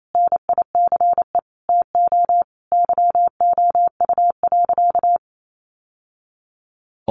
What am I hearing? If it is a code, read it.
Morse code, heard as a keyed tone, about 28 wpm, 700 Hz. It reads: NICE TO YOU.
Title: Sentences from Top 1100 Words 28wpm